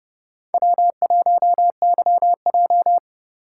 Morse code, heard as a keyed tone, about 33 wpm, 700 Hz. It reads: W1YJ